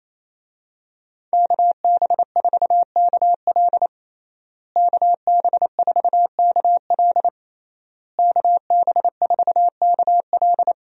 Morse code, heard as a keyed tone, about 28 wpm, 700 Hz. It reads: KB4KL KB4KL KB4KL
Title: USA Call Signs - 5 Characters - Repeated 3x - 28wpm